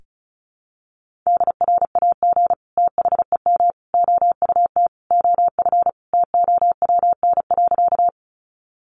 Morse code, heard as a keyed tone, about 34 wpm, 700 Hz. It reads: DRAG THEM OUT OF TOWN.
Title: Sentences from Top 1500 Words 35wpm